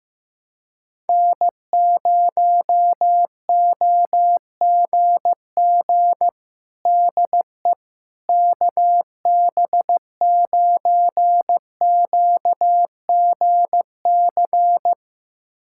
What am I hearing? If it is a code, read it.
Morse code, heard as a keyed tone, about 15 wpm, 700 Hz. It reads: N0OGG DE KB9QGC
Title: USA Callsign DE Callsign 15wpm